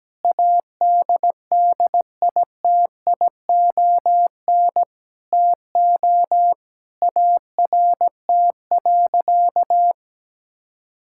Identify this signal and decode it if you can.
Morse code, heard as a keyed tone, about 17 wpm, 700 Hz. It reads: ADDITION TO ART.
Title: Sentences from Top 1000 Words - Review Part 6 - 17wpm